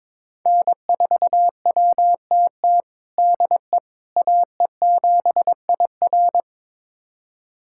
Morse code, heard as a keyed tone, about 22 wpm, 700 Hz. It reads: N4WTT DE AE7IR
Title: USA Callsign DE Callsign - 5 Characters - 22wpm